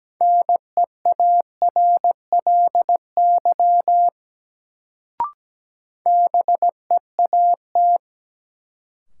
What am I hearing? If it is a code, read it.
Morse code, heard as a keyed tone, about 17 wpm, 700 Hz. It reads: NEARLY BEAT
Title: Sentences from Top 1300 Words - New Word Drill - 17wpm